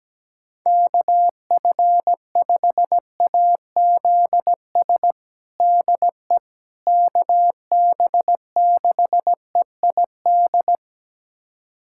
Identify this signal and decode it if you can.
Morse code, heard as a keyed tone, about 17 wpm, 700 Hz. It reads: KF5AZS DE KB6EID
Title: USA Callsign DE Callsign - 6 Characters - 17wpm